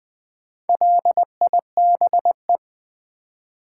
Morse code, heard as a keyed tone, about 20 wpm, 690 Hz. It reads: LIBE
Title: Build A Word - 4 Letters At A Time - 10-Letter Words - 20wpm